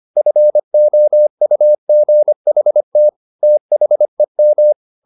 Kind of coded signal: Morse code